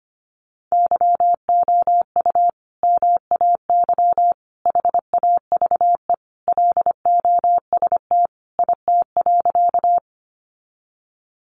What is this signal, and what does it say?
Morse code, heard as a keyed tone, about 25 wpm, 700 Hz. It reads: YOU MAY HAVE LOST IT.
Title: Sentences from Top 1400 Words 25wpm